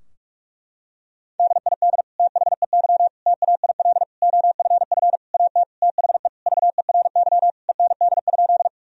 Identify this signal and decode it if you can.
Morse code, heard as a keyed tone, about 41 wpm, 700 Hz. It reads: DID THEY TAIL OFF AT THE VERY END?
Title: Sentences from Top 1000 Words - Review Part 3 - 45wpm